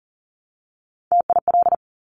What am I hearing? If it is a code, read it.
Morse code, heard as a keyed tone, about 38 wpm, 690 Hz. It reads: TIL
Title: Top Three Letter Combinations in Words 40wpm